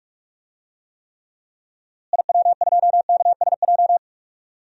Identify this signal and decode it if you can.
Morse code, heard as a keyed tone, about 45 wpm, 700 Hz. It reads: IW2KSJ
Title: DXCC-100 Call Signs - 6 Characters - Repeated 2x - 45wpm